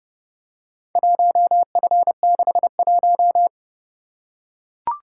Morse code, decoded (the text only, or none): 1F61